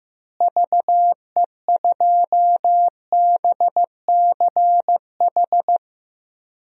Morse code, heard as a keyed tone, about 15 wpm, 700 Hz. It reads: VE2BCH
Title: DXCC-100 Call Signs - 6 Characters - Repeated 2x - 15wpm